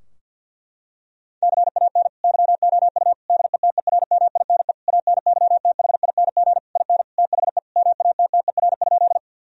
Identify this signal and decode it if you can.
Morse code, heard as a keyed tone, about 50 wpm, 700 Hz. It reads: CAN YOU DETERMINE ANYTHING IN THE MATTER?
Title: Sentences from Top 1000 Words - Review Part 4 - 50wpm